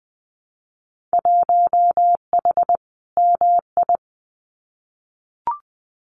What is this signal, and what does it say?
Morse code, heard as a keyed tone, about 20 wpm, 700 Hz. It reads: 1H MI
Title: Winter Field Day Exchanges 20wpm